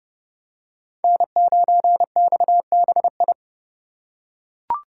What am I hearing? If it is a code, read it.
Morse code, heard as a keyed tone, about 29 wpm, 700 Hz. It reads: N9XBI